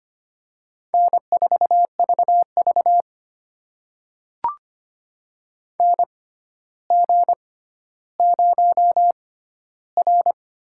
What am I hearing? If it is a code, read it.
Morse code, heard as a keyed tone, about 25 wpm, spaced as for 13 wpm, 700 Hz. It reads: N4VV NG0R
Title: USA Call Signs - 4 Characters - 6x Character Spacing - 25wpm